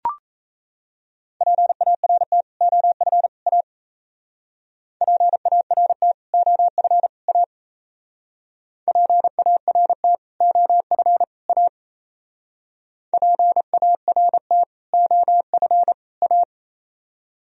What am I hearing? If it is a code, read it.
Morse code, heard as a keyed tone, about 35 wpm, 700 Hz. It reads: PART OF A PART OF A PART OF A PART OF A
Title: Speed-Racing - Sets of 3 Words 28wpm